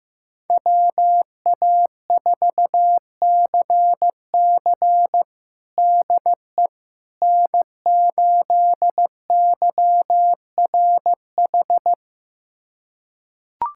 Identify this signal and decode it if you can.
Morse code, heard as a keyed tone, about 16 wpm, 700 Hz. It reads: WA4CC DE N8YRH
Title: USA Callsign DE Callsign - 5 Characters - 15wpm